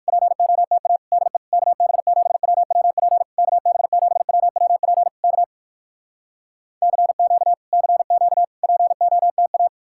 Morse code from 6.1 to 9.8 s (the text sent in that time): CQ CQ POTA